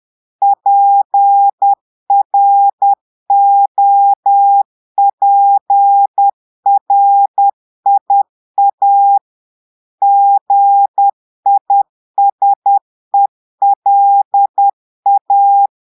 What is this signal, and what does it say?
Morse code, heard as a keyed tone, about 10 wpm, 800 Hz. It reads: PROPRIA GISELA
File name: Portugais - texte en clair 1 - vitesse - 10 mots minute